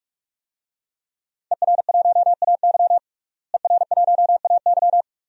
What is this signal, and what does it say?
Morse code, heard as a keyed tone, about 46 wpm, 700 Hz. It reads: ER1AY ER1AY
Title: DXCC-100 Call Signs - 5 Characters - Repeated 2x - 45wpm